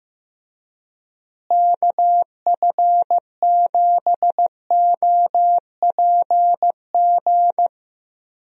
Morse code, heard as a keyed tone, about 16 wpm, 700 Hz. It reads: KF7OPG
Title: USA Call Signs - 6 Characters - Repeated 3x - 15wpm